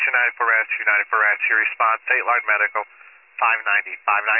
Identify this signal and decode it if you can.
Emergency Alert System (SAME) tones, heard as harsh buzzing bursts, with speech. It is heard from 1.2 to 4.0 s.